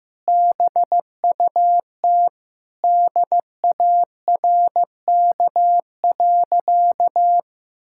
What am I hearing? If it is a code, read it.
Morse code, heard as a keyed tone, about 14 wpm, 700 Hz. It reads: BUT DARK.
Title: Sentences from Top 1600 Words 15wpm